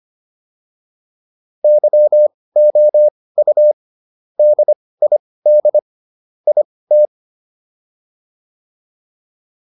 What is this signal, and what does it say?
Morse code, heard as a keyed tone, about 25 wpm, spaced as for 18 wpm, 600 Hz. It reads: YOU DID IT